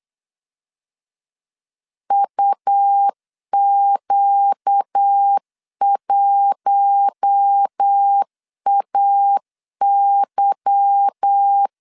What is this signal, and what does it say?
Morse code, heard as a keyed tone, about 8 wpm, 800 Hz. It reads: UQ1AY
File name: CW / DET / 1 9 1500C S C 8wpm 5653